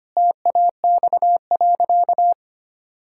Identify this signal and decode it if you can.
Morse code, heard as a keyed tone, about 25 wpm, 700 Hz. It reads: TAX.
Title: Sentences from Top 1000 Words - Review Part 6 - 25wpm